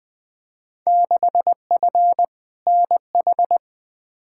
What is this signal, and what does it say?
Morse code, heard as a keyed tone, about 21 wpm, 700 Hz. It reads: 6F NH
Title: Field Day Exchanges Encore - 20wpm